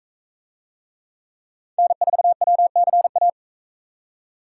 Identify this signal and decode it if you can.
Morse code, heard as a keyed tone, about 44 wpm, 700 Hz. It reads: N4WCA